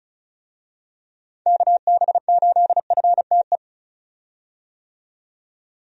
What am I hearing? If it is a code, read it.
Morse code, heard as a keyed tone, about 36 wpm, 700 Hz. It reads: KB8FTE
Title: USA Call Signs - 6 Characters - 35wpm